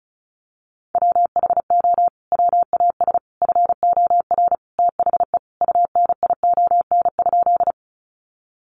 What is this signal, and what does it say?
Morse code, heard as a keyed tone, about 34 wpm, 700 Hz. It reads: WHO WAS FOR THE UNION?
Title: Sentences from Top 1400 Words 35wpm